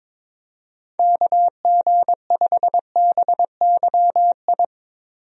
Morse code, heard as a keyed tone, about 22 wpm, 700 Hz. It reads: KG5BYI
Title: USA Call Signs - 6 Characters - Repeated 3x - 22wpm